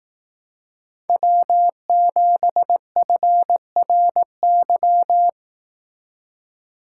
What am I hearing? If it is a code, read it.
Morse code, heard as a keyed tone, about 17 wpm, 700 Hz. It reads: W7FRY
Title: Speed Racing - USA 1x3 Call Signs 15wpm